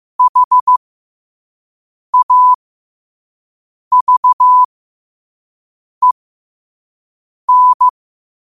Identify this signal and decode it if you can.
Morse code, heard as a keyed tone, about 15 wpm, spaced as for 5 wpm, 1000 Hz. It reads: HAVEN
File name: qotdfiles / 5wpm